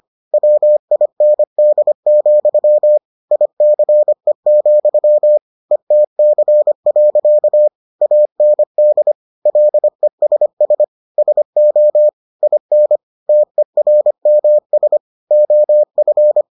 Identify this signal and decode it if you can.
Morse code, heard as a keyed tone, about 25 wpm, 600 Hz. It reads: WIND, ICE, ETC. AND LESS SO IN TERMS OF